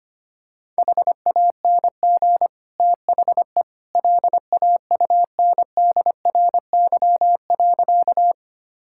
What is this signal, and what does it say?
Morse code, heard as a keyed tone, about 25 wpm, 700 Hz. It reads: HANG THE LAUNDRY.
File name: Easy Sentences with Hints - 25wpm